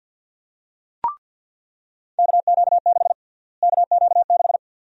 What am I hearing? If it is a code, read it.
Morse code, heard as a keyed tone, about 49 wpm, 700 Hz. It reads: KQ6 KQ6
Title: USA Call Sign Prefixes - Repeated 2x - 50wpm